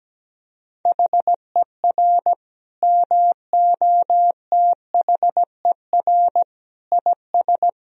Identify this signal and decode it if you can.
Morse code, heard as a keyed tone, about 17 wpm, 700 Hz. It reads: HER MOTHER IS
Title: Sentences from Top 1000 Words - Review Part 3 - 17wpm